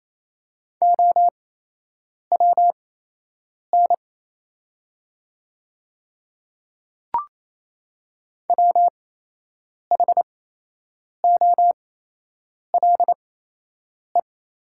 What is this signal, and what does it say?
Morse code, heard as a keyed tone, about 28 wpm, spaced as for 8 wpm, 700 Hz. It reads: OWN WHOLE
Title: 8x Character Spacing - Top 500 Words - 28wpm